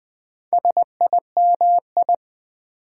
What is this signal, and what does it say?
Morse code, heard as a keyed tone, about 20 wpm, 700 Hz. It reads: SIMI
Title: Build A Word - 4 Letters At A Time - 10-Letter Words - 20wpm